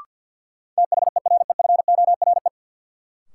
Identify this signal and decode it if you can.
Morse code, heard as a keyed tone, about 52 wpm, 700 Hz. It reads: THEREFORE
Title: Sentences from Top 800 Words - New Word Drill - 50wpm